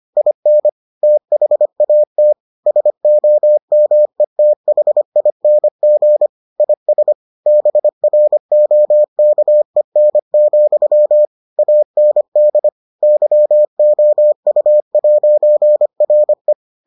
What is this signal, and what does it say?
Morse code, heard as a keyed tone, about 25 wpm, 600 Hz. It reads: IN THAT SOMETHING IS BROKEN, AND YOU'RE